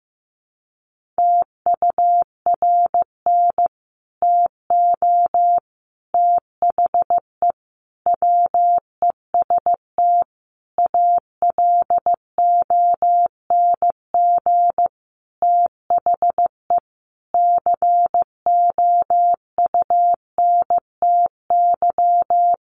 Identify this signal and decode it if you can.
Morse code, heard as a keyed tone, about 15 wpm, 700 Hz. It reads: TURN TO THE WEST ALONG THE COUNTY
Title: Sentences from Top 1200 Words 15wpm